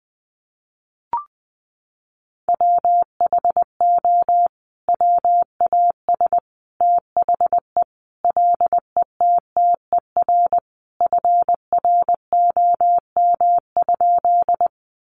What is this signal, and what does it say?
Morse code, heard as a keyed tone, about 19 wpm, 700 Hz. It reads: WHO WAS THE LETTER FROM?
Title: Sentences from Top 300 Words 20wpm